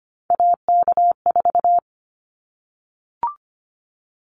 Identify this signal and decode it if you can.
Morse code, heard as a keyed tone, about 25 wpm, 700 Hz. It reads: AK4
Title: Callsign Prefixes for Top 25 DXCC - Repeated 3x - 25wpm